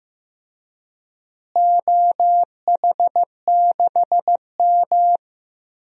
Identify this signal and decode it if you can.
Morse code, heard as a keyed tone, about 16 wpm, 700 Hz. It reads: OH6M
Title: DXCC-100 Call Signs - 4 Characters - Repeated 2x - 15wpm